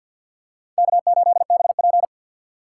Morse code, heard as a keyed tone, about 52 wpm, 700 Hz. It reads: K8BP
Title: USA Call Signs - 4 Characters - 2x Character Spacing - 50wpm